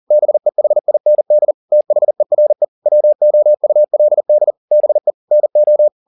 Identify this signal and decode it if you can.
Morse code, heard as a keyed tone, about 40 wpm, 600 Hz. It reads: BEHIND THERE WOULD BE NO
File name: the old man and the sea 2 40wpm